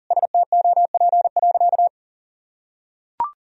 Morse code, heard as a keyed tone, about 40 wpm, 700 Hz. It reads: STOP.
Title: Sentences from Top 1000 Words - Review Part 2 - 40wpm